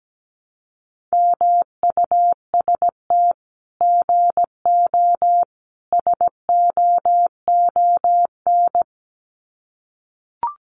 Morse code, heard as a keyed tone, about 17 wpm, 700 Hz. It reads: MUST GO SOON
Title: Common QSO Phrases 17wpm